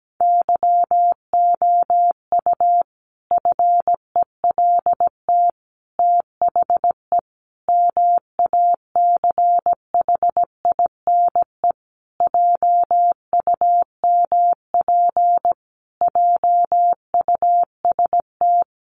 Morse code, heard as a keyed tone, about 17 wpm, 700 Hz. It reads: YOU FELT THE MACHINE JUMP JUST
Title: A Sound Of Thunder 17wpm